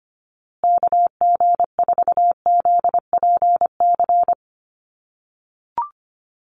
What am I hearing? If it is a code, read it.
Morse code, heard as a keyed tone, about 25 wpm, 700 Hz. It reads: KG4ZPC
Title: USA Call Signs Repeated 2x 25wpm